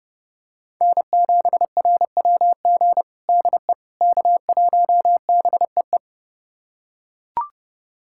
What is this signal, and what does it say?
Morse code, heard as a keyed tone, about 30 wpm, 700 Hz. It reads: N7RWG DE K1BEE
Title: USA Callsign DE Callsign - 5 Characters - 30wpm